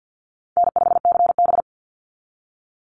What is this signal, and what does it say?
Morse code, heard as a keyed tone, about 48 wpm, 700 Hz. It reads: N5CB
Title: USA Call Signs 50wpm